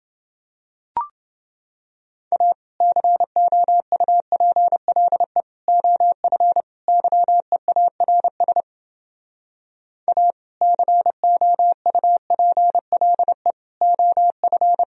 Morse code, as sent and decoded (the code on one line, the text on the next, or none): .- / -.-. --- ..- .--. .-.. / --- ..-. / -.-- . .- .-. ... / .- / -.-. --- ..- .--. .-.. / --- ..-.
A COUPLE OF YEARS A COUPLE OF